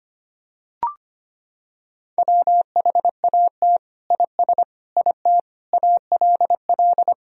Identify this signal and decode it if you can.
Morse code, heard as a keyed tone, about 25 wpm, 700 Hz. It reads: WHAT IS IT ALL
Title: Sentences from Top 100 Words and Abbreviations 25wpm